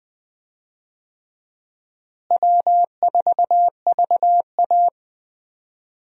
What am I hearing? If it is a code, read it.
Morse code, heard as a keyed tone, about 20 wpm, 700 Hz. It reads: W4VA
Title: USA Call Signs - 4 Characters - Repeated 3x - 20wpm